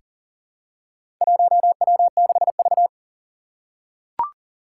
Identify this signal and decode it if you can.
Morse code, heard as a keyed tone, about 40 wpm, 700 Hz. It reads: 1W6V
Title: Four Challenging Letter-Number Combinations 40wpm